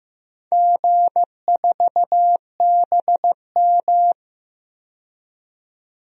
Morse code, heard as a keyed tone, about 15 wpm, 700 Hz. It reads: G4BM
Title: DXCC-100 Call Signs - 4 Characters - 15wpm